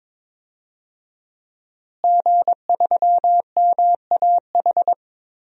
Morse code, heard as a keyed tone, about 22 wpm, 700 Hz. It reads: G3MAH
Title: DXCC-100 Call Signs - 5 Characters - Repeated 2x - 22wpm